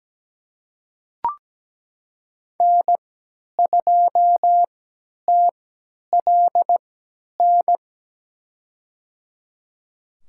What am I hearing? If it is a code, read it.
Morse code, heard as a keyed tone, about 17 wpm, spaced as for 10 wpm, 700 Hz. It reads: N2TLN
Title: USA Call Signs - 5 Characters - 3x Character Spacing - 17wpm